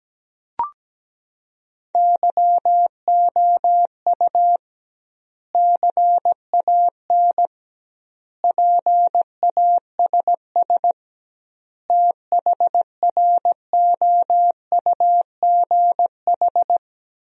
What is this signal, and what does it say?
Morse code, heard as a keyed tone, about 17 wpm, 700 Hz. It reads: YOU CAN PASS THROUGH
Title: 2x Word-Spacing - Sentences from Top 400 Words - 17wpm